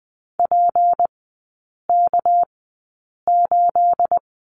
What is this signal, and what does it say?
Morse code, heard as a keyed tone, about 20 wpm, spaced as for 8 wpm, 690 Hz. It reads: PK8
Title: 212 Lesson 34e - Number 8 - Three Characters 20wpm